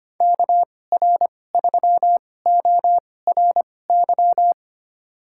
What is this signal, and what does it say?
Morse code, heard as a keyed tone, about 25 wpm, spaced as for 18 wpm, 700 Hz. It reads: KR3ORY
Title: USA Call Signs - 6 Characters - 2x Character Spacing - 25wpm